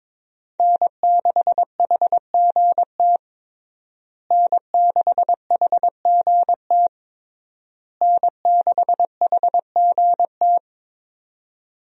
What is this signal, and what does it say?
Morse code, heard as a keyed tone, about 22 wpm, 700 Hz. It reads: N6HGT N6HGT N6HGT
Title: USA Call Signs - 5 Characters - Repeated 3x - 22wpm